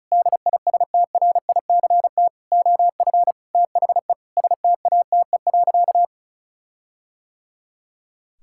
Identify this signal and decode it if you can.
Morse code, heard as a keyed tone, about 36 wpm, 700 Hz. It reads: DISTRICT OF THE STATE.
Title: Sentences from Top 1600 Words 35wpm